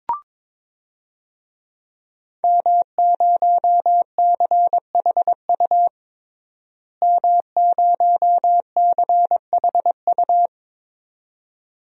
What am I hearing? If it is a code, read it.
Morse code, heard as a keyed tone, about 22 wpm, 700 Hz. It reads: M0CHU M0CHU
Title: DXCC-100 Call Signs - 5 Characters - Repeated 2x - 22wpm